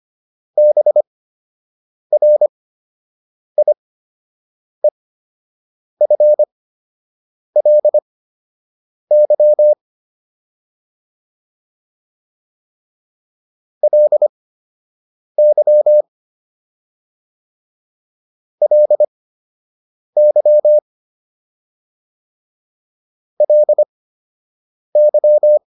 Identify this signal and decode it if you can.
Morse code, heard as a keyed tone, about 25 wpm, spaced as for 7 wpm, 600 Hz. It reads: BRIEFLY LY LY LY